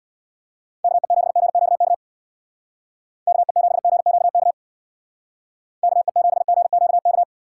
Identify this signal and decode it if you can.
Morse code, heard as a keyed tone, about 60 wpm, 700 Hz. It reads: KE7GQK KE7GQK KE7GQK
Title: Speed Racing - USA Call Signs 50wpm